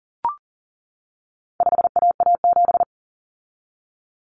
Morse code, heard as a keyed tone, about 40 wpm, 700 Hz. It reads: 5AA7
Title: Common Cut Numbers - 4-Digit 40wpm